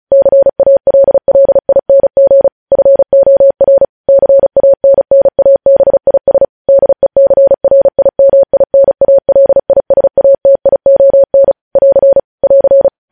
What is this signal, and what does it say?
Morse code, heard as a keyed tone, about 36 wpm, 550 Hz. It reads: CALLING FOR CANNABIS DECRIMINALISATION <AR> <AR>
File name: podcast / News headlines-1x-35x35